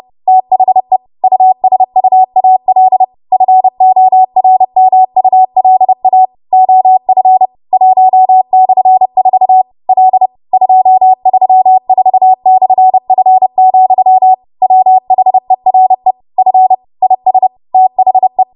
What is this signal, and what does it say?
Morse code, heard as a keyed tone, about 30 wpm, 750 Hz. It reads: THE USUAL FORMULA OF 1/4 L 234/F, WHERE F IS THE
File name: Morse / 30WPM